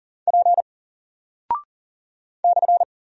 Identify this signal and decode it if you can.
Morse code, heard as a keyed tone, about 40 wpm, 700 Hz. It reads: P /
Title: Single Character (Letter:Number:Punctuation:Prosign) Alt 40wpm